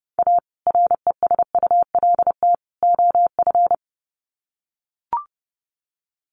Morse code, heard as a keyed tone, about 30 wpm, 700 Hz. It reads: A RESULT OF
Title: Sets of 3 Words 30wpm